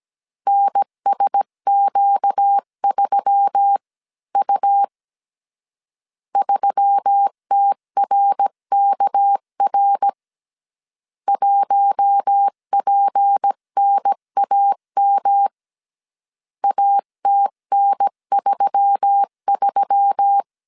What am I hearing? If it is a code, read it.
Morse code, heard as a keyed tone, about 17 wpm, 800 Hz. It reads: NSQ3U 3TRKR 1PNAM ATN33
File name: CW / DET / 1 6 1500C S C 16wpm 2733